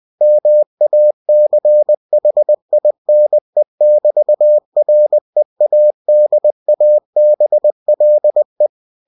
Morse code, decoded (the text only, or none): MACHINE<BT>READABLE